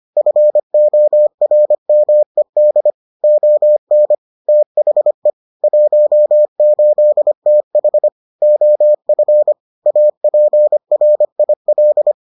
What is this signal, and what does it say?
Morse code, heard as a keyed tone, about 25 wpm, 600 Hz. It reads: FORMED ON THE 18TH OF APRIL